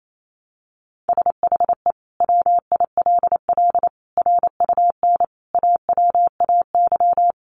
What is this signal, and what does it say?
Morse code, heard as a keyed tone, about 28 wpm, 700 Hz. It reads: SHE WILL RUN AWAY
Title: Sentences from Top 600 Words 28wpm